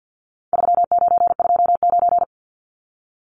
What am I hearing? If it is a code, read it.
Morse code, heard as a keyed tone, about 48 wpm, 700 Hz. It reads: 3929
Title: Numbers - 4-Digit 50wpm